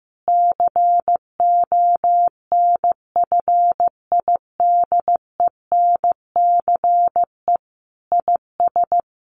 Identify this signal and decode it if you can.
Morse code, heard as a keyed tone, about 15 wpm, 700 Hz. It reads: CONFIDENCE IS
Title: Sentences from Top 1500 Words 15wpm